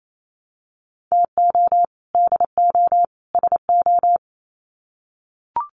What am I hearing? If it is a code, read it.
Morse code, heard as a keyed tone, about 28 wpm, 700 Hz. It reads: TO DO SO
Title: Sets of 3 Words 28wpm